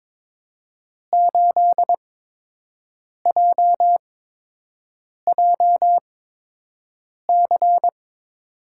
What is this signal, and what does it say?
Morse code, heard as a keyed tone, about 21 wpm, spaced as for 6 wpm, 700 Hz. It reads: 8JJC
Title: USA Call Signs - 5 Characters - 8x Character Spacing - 22wpm